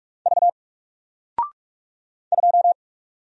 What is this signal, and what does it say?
Morse code, heard as a keyed tone, about 44 wpm, 700 Hz. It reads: V 2
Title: Single Letter-Number 45wpm